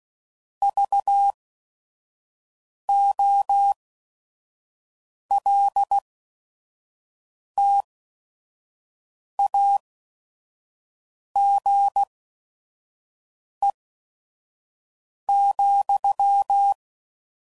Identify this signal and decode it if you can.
Morse code, heard as a keyed tone, about 16 wpm, spaced as for 5 wpm, 800 Hz. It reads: VOLTAGE,
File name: CW Code Groups @5WPM / Technical Subjects / XC